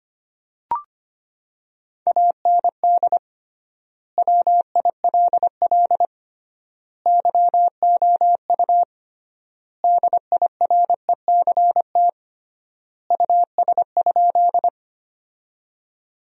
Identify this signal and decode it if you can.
Morse code, heard as a keyed tone, about 25 wpm, 700 Hz. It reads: AND WILL YOU DIRECT US?
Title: 3x Word-Spacing - Sentences from Top 400 Words - 25wpm